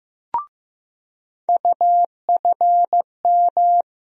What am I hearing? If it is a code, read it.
Morse code, heard as a keyed tone, about 15 wpm, 700 Hz. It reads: UFM
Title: USA Call Sign Postfixes - 3 Letters 15wpm